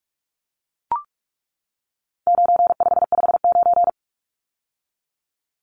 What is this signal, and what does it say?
Morse code, heard as a keyed tone, about 44 wpm, 700 Hz. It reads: 9559
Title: Numbers - 4-Digit 45wpm